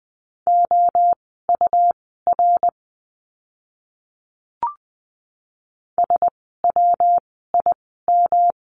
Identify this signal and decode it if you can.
Morse code, heard as a keyed tone, about 20 wpm, spaced as for 15 wpm, 700 Hz. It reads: OUR SWIM
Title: 115 Lesson 19f - Letter w - Words 20wpm